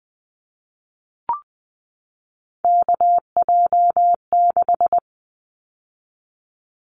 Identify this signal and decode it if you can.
Morse code, heard as a keyed tone, about 20 wpm, 700 Hz. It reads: KJ6
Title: Callsign Prefixes for Top 25 DXCC - 20wpm